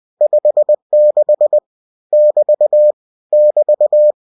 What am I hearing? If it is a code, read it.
Morse code, heard as a keyed tone, about 20 wpm, 600 Hz. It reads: 56 <BT> <BT>
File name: cw-books / winnie-the-pooh / winnie-20wpm-page-0056